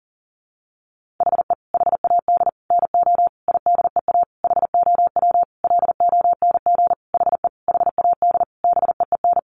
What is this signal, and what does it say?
Morse code, heard as a keyed tone, about 40 wpm, 700 Hz. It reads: HE HAD NO IDEA HOW LONG HE HAD BEEN
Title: Sentences from Top 300 Words 40wpm